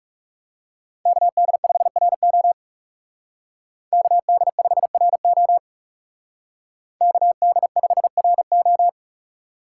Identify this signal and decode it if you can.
Morse code, heard as a keyed tone, about 35 wpm, 700 Hz. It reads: KD5RO KD5RO KD5RO
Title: Speed Racing - USA 2x2 Call Signs 30wpm